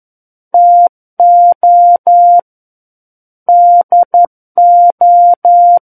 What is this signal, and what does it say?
Morse code, heard as a keyed tone, about 11 wpm, 700 Hz. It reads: TO DO
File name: 11 WPM